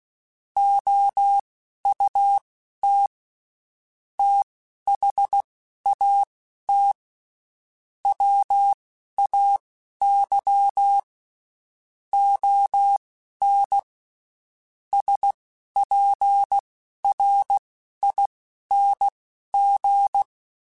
Morse code, heard as a keyed tone, about 16 wpm, spaced as for 11 wpm, 800 Hz. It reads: OUT THAT WAY ON SPRING
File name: (spaced) / QSOs Spaced-mp3s / WA7QOX 1 @15 sp